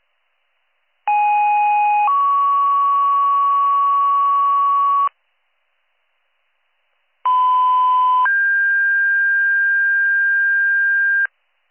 Two-tone page, heard as a one-second tone, then a three-second tone, about 830 then 1120 Hz, about 1010 then 1600 Hz. It is heard from 1.1 to 11.3 s.